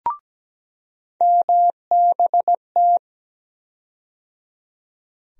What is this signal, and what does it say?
Morse code, heard as a keyed tone, about 17 wpm, 700 Hz. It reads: MBT
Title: USA Call Sign Postfixes - 3 Letters 17wpm